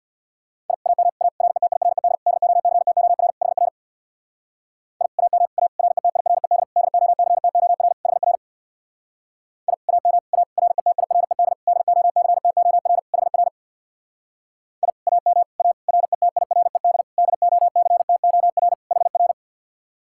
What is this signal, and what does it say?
Morse code, heard as a keyed tone, about 62 wpm, 700 Hz. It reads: I AM A RETIRED DOCTOR HR I AM A RETIRED DOCTOR HR I AM A RETIRED DOCTOR HR I AM A RETIRED DOCTOR HR
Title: Speed-Racing Common QSO Phrases - 50wpm